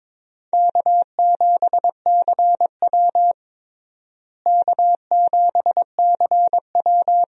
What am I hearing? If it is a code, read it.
Morse code, heard as a keyed tone, about 22 wpm, 700 Hz. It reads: K7CW K7CW
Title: USA Call Signs - 4 Characters - Repeated 2x - 22wpm